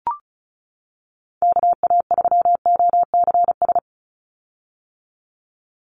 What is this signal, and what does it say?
Morse code, heard as a keyed tone, about 34 wpm, 700 Hz. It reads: KA3OCS
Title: Call Signs - 35wpm